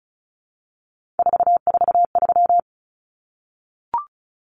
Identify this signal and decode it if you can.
Morse code, heard as a keyed tone, about 34 wpm, 700 Hz. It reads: 443
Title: Common Cut Numbers - 3-Digit 35wpm